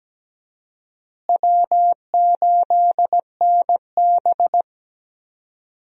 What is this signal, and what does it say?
Morse code, heard as a keyed tone, about 17 wpm, 700 Hz. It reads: W8NB
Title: USA Call Signs - 4 Characters - Repeated 2x - 17wpm